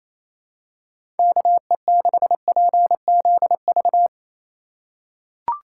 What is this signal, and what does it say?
Morse code, heard as a keyed tone, about 28 wpm, 700 Hz. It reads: KE6PZV
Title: USA Call Signs - 6 Characters - 3x Character Spacing - 28wpm